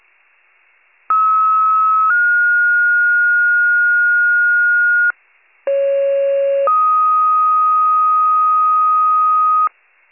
Two-tone page, heard as a one-second tone, then a three-second tone, about 1290 then 1430 Hz, about 550 then 1250 Hz. It is heard from 1.1 to 9.7 s.